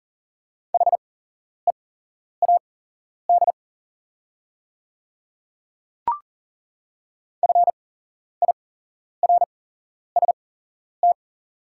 Morse code, heard as a keyed tone, about 38 wpm, spaced as for 11 wpm, 700 Hz. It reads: HEAD FIRST
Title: 8x Character Spacing - Top 300 Words - 40wpm